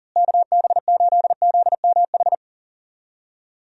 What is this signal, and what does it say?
Morse code, heard as a keyed tone, about 37 wpm, 700 Hz. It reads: KB8ZMH
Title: USA Call Signs - 6 Characters - Repeated 3x - 40wpm